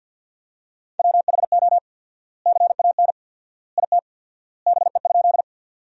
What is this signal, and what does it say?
Morse code, heard as a keyed tone, about 53 wpm, 700 Hz. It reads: WHO CAN IT BE?
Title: 4x Word-Spacing - Sentences from Top 100 Words - 50wpm